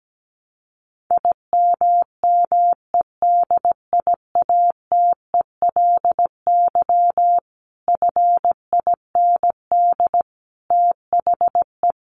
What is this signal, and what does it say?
Morse code, heard as a keyed tone, about 17 wpm, 700 Hz. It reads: IMMEDIATELY FIND THE
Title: Sentences from Top 1200 Words 17wpm